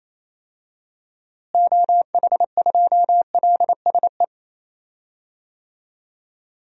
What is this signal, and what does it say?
Morse code, heard as a keyed tone, about 28 wpm, 700 Hz. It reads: OH2LSE
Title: DXCC-100 Call Signs - 6 Characters - Repeated 3x - 28wpm